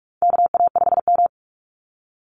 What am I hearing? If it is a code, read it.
Morse code, heard as a keyed tone, about 44 wpm, 700 Hz. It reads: KA5M
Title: Call Signs Repeated 2x - 45wpm